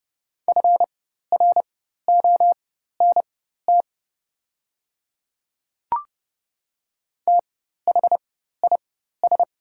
Morse code, heard as a keyed tone, about 30 wpm, spaced as for 14 wpm, 690 Hz. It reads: FRONT THIS
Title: 4x Character Spacing - Top 500 Words - 30wpm